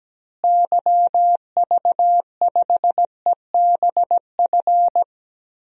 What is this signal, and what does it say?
Morse code, heard as a keyed tone, about 17 wpm, 700 Hz. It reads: YV5EBF
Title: DXCC-100 Call Signs - 6 Characters - Repeated 2x - 17wpm